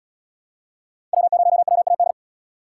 Morse code, heard as a keyed tone, about 77 wpm, 700 Hz. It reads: W0WAD